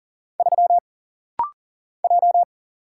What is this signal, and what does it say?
Morse code, heard as a keyed tone, about 40 wpm, 700 Hz. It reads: 3 J
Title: Single Letter-Number Alt - Rapid-Fire 40wpm